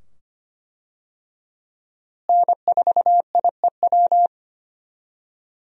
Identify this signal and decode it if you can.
Morse code, heard as a keyed tone, about 25 wpm, 700 Hz. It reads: N4IEW
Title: USA Call Signs - 5 Characters - Repeated 2x - 25wpm